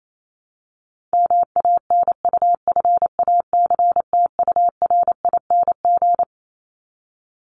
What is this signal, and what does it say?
Morse code, heard as a keyed tone, about 28 wpm, 700 Hz. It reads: MANUFACTURING
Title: ING Suffix - Top 100 Words with Base Words 28wpm